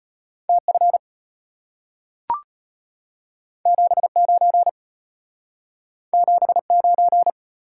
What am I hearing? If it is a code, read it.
Morse code, heard as a keyed tone, about 33 wpm, 700 Hz. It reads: TF 79 79